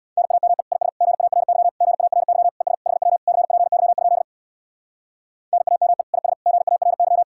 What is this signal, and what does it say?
Morse code, heard as a keyed tone, about 60 wpm, 700 Hz. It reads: NAME IS GARY GARY ES HW CPY? NAME IS GARY
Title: Speed-Racing Common QSO Phrases - 50wpm